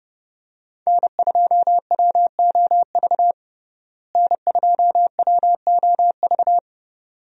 Morse code, heard as a keyed tone, about 30 wpm, 700 Hz. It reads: N2WOV N2WOV
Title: USA Call Signs - 5 Characters - Repeated 2x - 30wpm